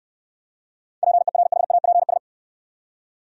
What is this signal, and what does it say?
Morse code, heard as a keyed tone, about 68 wpm, 700 Hz. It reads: PERHAPS